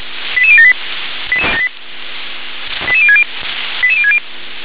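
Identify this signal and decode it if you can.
Five-tone selective call (ZVEI), heard as quick runs of five tones, tones of about 70 ms. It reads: 88078 88060 88060
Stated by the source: ZVEI five-tone call received by SDR.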